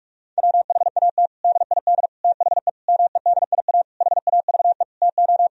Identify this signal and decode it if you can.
Morse code, heard as a keyed tone, about 46 wpm, 700 Hz. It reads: WHAT DID THE MEDIA HAVE TO
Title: Sentences from Top 1000 Words - Review Part 4 - 45wpm